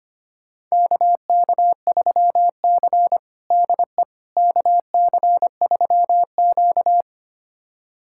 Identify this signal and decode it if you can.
Morse code, heard as a keyed tone, about 25 wpm, 700 Hz. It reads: KK3C DE KC3Q
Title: USA Callsign DE Callsign - 4 Characters - 25wpm